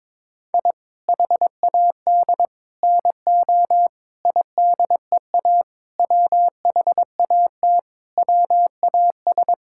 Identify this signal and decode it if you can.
Morse code, heard as a keyed tone, about 21 wpm, 700 Hz. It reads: I HAD NO IDEA WHAT WAS
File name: practice / 22wpm / Sentences with 8-Word Phrases - 22wpm